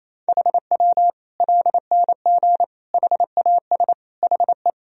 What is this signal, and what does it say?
Morse code, heard as a keyed tone, about 28 wpm, 700 Hz. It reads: HW LNG HAS HE
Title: Sentences from Top 100 Words and Abbreviations 28wpm